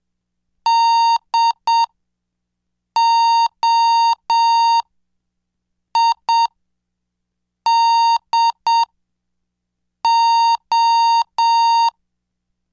Morse code, heard as a keyed tone, about 7 wpm, spaced as for 5 wpm, 920 Hz. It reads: DOIDO